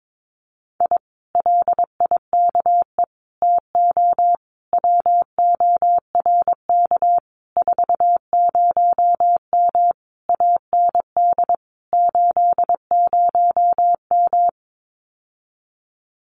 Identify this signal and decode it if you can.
Morse code, heard as a keyed tone, about 22 wpm, 700 Hz. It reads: I LIKE TO WORK 40M AND 80M
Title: Common QSO Phrases Encore 22wpm